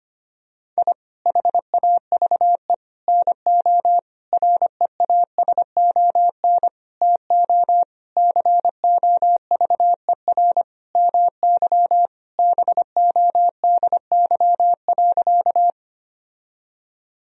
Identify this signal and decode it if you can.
Morse code, heard as a keyed tone, about 25 wpm, 700 Hz. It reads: I HAVE NO REASON TO COVER MY BODY.
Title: Sentences from Top 1000 Words - Review Part 1 - 25wpm